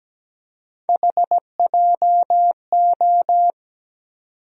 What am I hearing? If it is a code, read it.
Morse code, heard as a keyed tone, about 17 wpm, 700 Hz. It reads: HJO
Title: USA Call Sign Postfixes - 3 Letters 17wpm